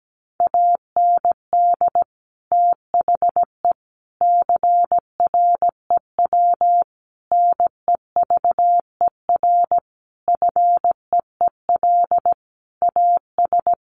Morse code, heard as a keyed tone, about 17 wpm, 700 Hz. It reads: AND THE CREW NEVER FEEL AS
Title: Sentences from Top 1500 Words 17wpm